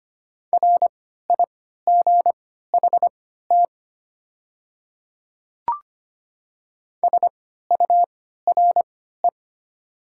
Morse code, heard as a keyed tone, about 25 wpm, spaced as for 14 wpm, 700 Hz. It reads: RIGHT SURE
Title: 3x Character Spacing - Top 300 Words - 25wpm